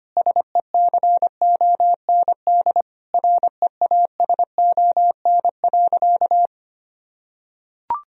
Morse code, heard as a keyed tone, about 25 wpm, 700 Hz. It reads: SECOND REASON.
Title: Sentences from Top 1000 Words - Review Part 1 - 25wpm